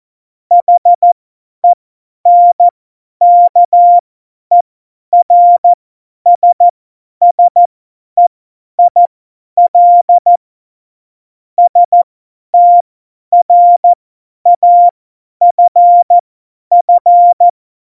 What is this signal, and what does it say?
Morse code, heard as a keyed tone, about 14 wpm, spaced as for 10 wpm, 700 Hz. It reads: HENKERSSEIL STRAFF
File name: Morsekurs / Witze / Witz 17. November 2020, 50 F70